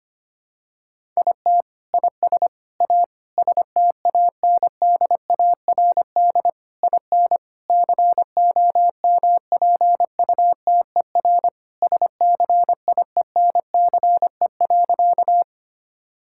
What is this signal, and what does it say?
Morse code, heard as a keyed tone, about 25 wpm, 700 Hz. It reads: IT IS A STANDARD IN COMPUTER SCIENCE.
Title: Sentences from Top 1000 Words - Review Part 6 - 25wpm